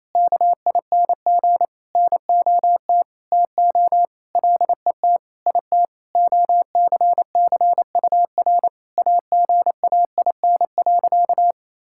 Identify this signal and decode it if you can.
Morse code, heard as a keyed tone, about 28 wpm, 700 Hz. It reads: KING NOT TO LET IT OCCUR AGAIN.
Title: Sentences from Top 1000 Words - Review Part 6 - 28wpm